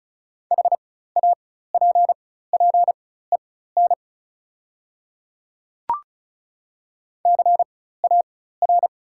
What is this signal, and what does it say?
Morse code, heard as a keyed tone, about 35 wpm, spaced as for 16 wpm, 690 Hz. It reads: HAPPEN CAR
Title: 4x Character Spacing - Top 500 Words - 35wpm